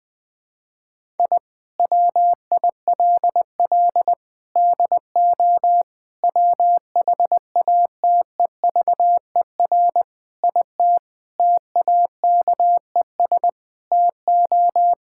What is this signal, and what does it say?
Morse code, heard as a keyed tone, about 20 wpm, 700 Hz. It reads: I WILL DO WHATEVER IT TAKES TO
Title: Sentences with 6-Word Phrases - 20wpm